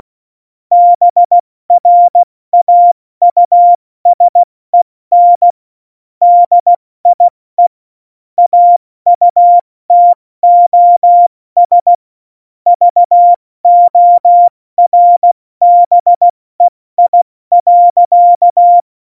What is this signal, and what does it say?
Morse code, heard as a keyed tone, about 16 wpm, 700 Hz. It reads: BRAUSEN DIE AUTOS VORBEI.
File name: Morsekurs / Witze / Witz 31. Oktober 17, Tempo 70